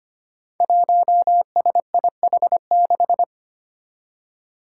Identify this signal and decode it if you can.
Morse code, heard as a keyed tone, about 25 wpm, 700 Hz. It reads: 1SIH6
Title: Five Challenging Letter-Number Combinations 25wpm